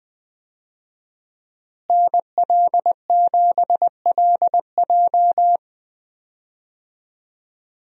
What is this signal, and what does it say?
Morse code, heard as a keyed tone, about 21 wpm, 700 Hz. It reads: NL7LJ
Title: USA Call Signs - 5 Characters - Repeated 2x - 20wpm